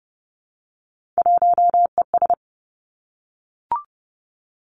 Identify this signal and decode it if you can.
Morse code, heard as a keyed tone, about 30 wpm, 700 Hz. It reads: 1ES
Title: Three Challenging Letter-Number Combinations 30wpm